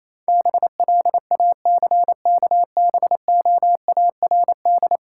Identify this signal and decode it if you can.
Morse code, heard as a keyed tone, about 28 wpm, 690 Hz. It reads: BLACKBOARD